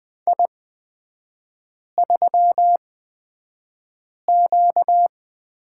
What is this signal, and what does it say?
Morse code, heard as a keyed tone, about 20 wpm, spaced as for 5 wpm, 700 Hz. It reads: I3Q
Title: 236 Lesson 38e - Letter q - Three Characters 20wpm